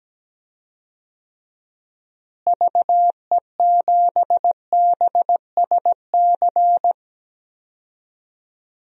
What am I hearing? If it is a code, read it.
Morse code, heard as a keyed tone, about 17 wpm, 700 Hz. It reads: VE7BSC
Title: DXCC-100 Call Signs - 6 Characters - Repeated 2x - 17wpm